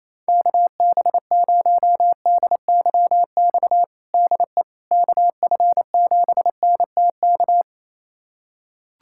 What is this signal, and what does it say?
Morse code, heard as a keyed tone, about 28 wpm, 700 Hz. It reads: KB0DYX DE KF7NTK
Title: USA Callsign DE Callsign - 6 Characters - 28wpm